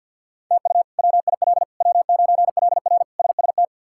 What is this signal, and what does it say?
Morse code, heard as a keyed tone, about 51 wpm, 700 Hz. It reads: TU WIL W9LR SST
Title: K1USN SST QSO Practice - 50wpm